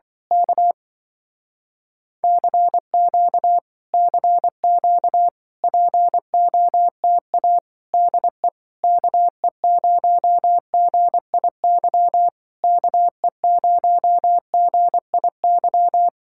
Morse code, heard as a keyed tone, about 25 wpm, 700 Hz. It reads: K CQ CQ POTA DE KE0GIY KE0GIY